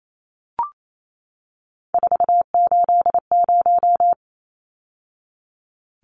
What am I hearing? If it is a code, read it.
Morse code, heard as a keyed tone, about 28 wpm, 700 Hz. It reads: 480
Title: Numbers - 3-Digit 28wpm